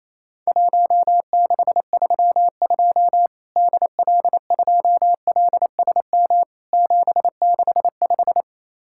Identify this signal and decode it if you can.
Morse code, heard as a keyed tone, about 28 wpm, 700 Hz. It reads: 1632 DL2LSM 765
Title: Worked All Europe DX Contest - QTC Exchange - 28wpm